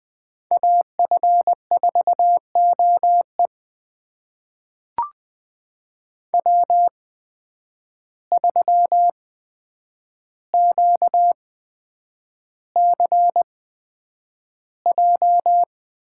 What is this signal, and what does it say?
Morse code, heard as a keyed tone, about 21 wpm, spaced as for 9 wpm, 700 Hz. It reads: AF4OE W3QCJ
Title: USA Call Signs - 5 Characters - 8x Character Spacing - 20wpm